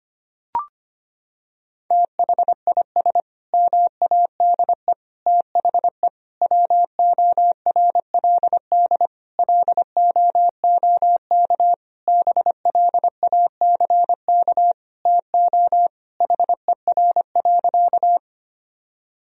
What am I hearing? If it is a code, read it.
Morse code, heard as a keyed tone, about 25 wpm, 700 Hz. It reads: THIS MADE THE WORLD LOOK BLACK TO HER.
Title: Sentences from Top 1000 Words - Review Part 2 - 25wpm